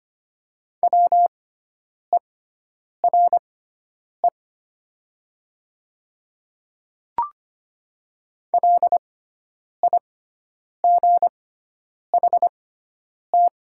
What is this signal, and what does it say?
Morse code, heard as a keyed tone, about 25 wpm, spaced as for 9 wpm, 700 Hz. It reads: WERE LIGHT
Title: 6x Character Spacing - Top 200 Words - 25wpm